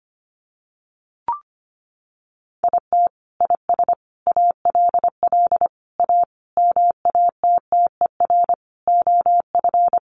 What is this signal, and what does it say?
Morse code, heard as a keyed tone, about 25 wpm, 700 Hz. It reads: IT IS ALL A MATTER OF
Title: Sentences from Top 1300 Words 25wpm